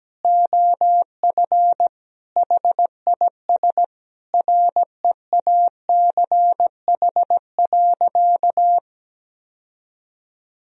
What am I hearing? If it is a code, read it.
Morse code, heard as a keyed tone, about 17 wpm, 700 Hz. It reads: OF HIS REACH.
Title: Sentences from Top 1000 Words - Review Part 2 - 17wpm